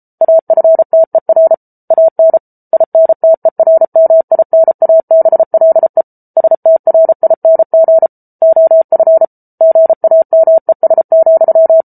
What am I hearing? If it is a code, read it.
Morse code, heard as a keyed tone, about 34 wpm, 650 Hz. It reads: AFTER AN INTERMINABLE STRING OF GAMES,